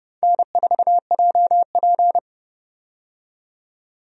Morse code, heard as a keyed tone, about 29 wpm, 700 Hz. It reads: N4JP
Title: USA Call Signs - 4 Characters - 30wpm